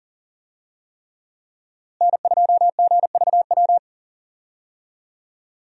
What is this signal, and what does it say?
Morse code, heard as a keyed tone, about 40 wpm, 700 Hz. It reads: N2GVW
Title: USA Call Signs - 5 Characters - Repeated 3x - 40wpm